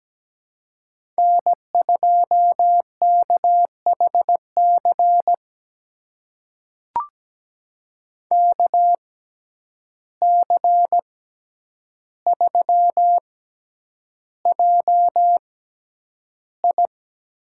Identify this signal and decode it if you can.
Morse code, heard as a keyed tone, about 17 wpm, spaced as for 9 wpm, 700 Hz. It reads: N2KHC KC3JI
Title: USA Call Signs - 6x Character Spacing - 17wpm